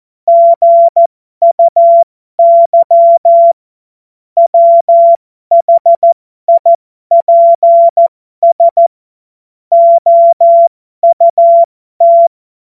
Morse code, heard as a keyed tone, about 14 wpm, spaced as for 12 wpm, 680 Hz. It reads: GUY WHIPS OUT
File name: Morsekurs / Witze